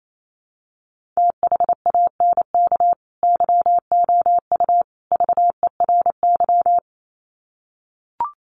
Morse code, heard as a keyed tone, about 28 wpm, 700 Hz. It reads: THANK YOU VERY
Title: Sets of 3 Words 28wpm